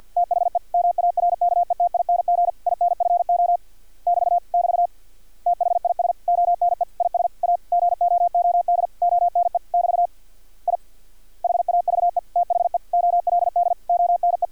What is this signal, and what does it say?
Morse code, heard as a keyed tone, about 50 wpm, 700 Hz. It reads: THE MARKETING INFO <BT> <BT> THIS ONE IS A GOOD ONE <BT> I HAVE THE OLD ONE
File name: archived / CRSnet